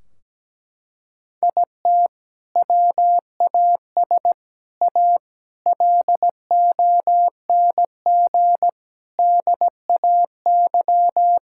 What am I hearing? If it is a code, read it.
Morse code, heard as a keyed tone, about 17 wpm, 700 Hz. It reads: IT WAS A LONG DAY
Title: Sentences from Top 1600 Words 17wpm